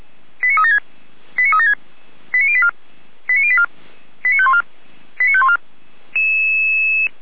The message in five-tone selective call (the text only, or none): 89376 89376 89084 89084 89514 89514